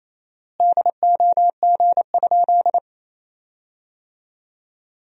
Morse code, heard as a keyed tone, about 28 wpm, 700 Hz. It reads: DOG?
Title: 4x Word-Spacing - Sentences from Top 400 Words - 28wpm